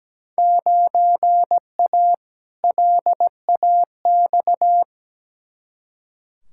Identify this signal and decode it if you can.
Morse code, heard as a keyed tone, about 17 wpm, 700 Hz. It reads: 9A LAX
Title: Field Day Exchanges Encore - 17wpm